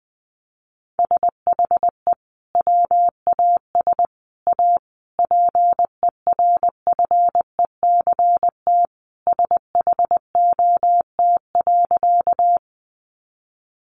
Morse code, heard as a keyed tone, about 20 wpm, 700 Hz. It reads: SHE WAS A PERFECT SHOT.
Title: Sentences from Top 1100 Words 20wpm